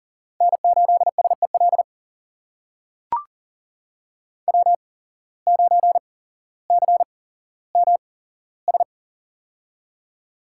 Morse code, heard as a keyed tone, about 40 wpm, spaced as for 17 wpm, 700 Hz. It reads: N8SEL W9CMS
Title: USA Call Signs - 5 Characters - 8x Character Spacing - 40wpm